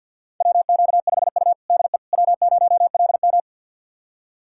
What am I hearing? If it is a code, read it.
Morse code, heard as a keyed tone, about 51 wpm, 700 Hz. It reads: WQ5U DE W0LM
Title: USA Callsign DE Callsign - 4 Characters - 50wpm